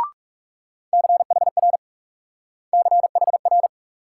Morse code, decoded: CHR CHR

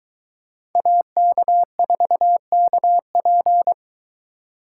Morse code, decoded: AK4KP

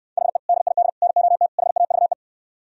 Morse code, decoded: HE DID NOT HAVE

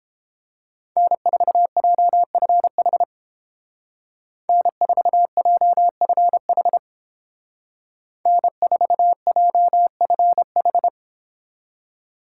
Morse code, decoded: N4JFH N4JFH N4JFH

